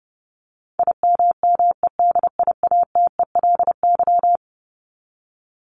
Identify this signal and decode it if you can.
Morse code, heard as a keyed tone, about 30 wpm, 700 Hz. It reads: IMMEDIATELY
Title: LY Suffix - Top 100 Words with Base Words 30wpm